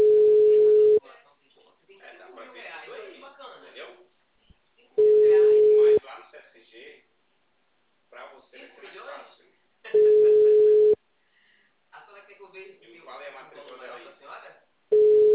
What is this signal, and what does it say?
Ringback tone (Europe, 425 Hz), with speech.